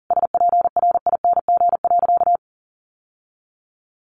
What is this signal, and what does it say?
Morse code, heard as a keyed tone, about 40 wpm, 700 Hz. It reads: SPRING.
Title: Sentences from Top 1200 Words 40wpm